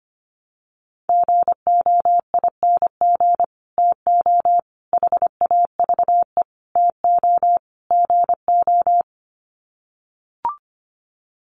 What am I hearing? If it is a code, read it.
Morse code, heard as a keyed tone, about 25 wpm, 700 Hz. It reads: GOING TO HAVE TO GO
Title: Sets of 5 Words Encore 25wpm